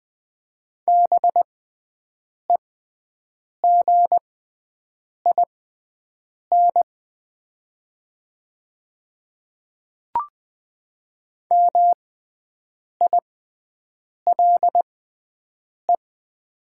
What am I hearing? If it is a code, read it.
Morse code, heard as a keyed tone, about 20 wpm, spaced as for 7 wpm, 700 Hz. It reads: BEGIN MILE
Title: 6x Character Spacing - Top 300 Words - 20wpm